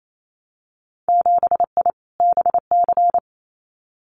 Morse code, heard as a keyed tone, about 28 wpm, 700 Hz. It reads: 7I BC
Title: Winter Field Day Exchanges Encore 28wpm